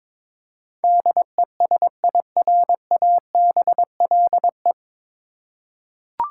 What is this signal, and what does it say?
Morse code, heard as a keyed tone, about 22 wpm, 700 Hz. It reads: DESIRABLE
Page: Morse Code Ninja practice 22wpm